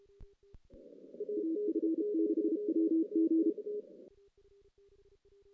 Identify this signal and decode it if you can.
Morse code, heard as a keyed tone, about 33 wpm, 320 Hz. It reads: AF6WG
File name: 2016 JAN CW NAQP / 2016 Jan naqp k5ot - 0-6